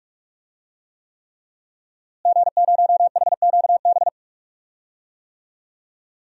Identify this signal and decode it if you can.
Morse code, heard as a keyed tone, about 42 wpm, 700 Hz. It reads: G0HQB
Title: DXCC-100 Call Signs - 5 Characters - Repeated 3x - 45wpm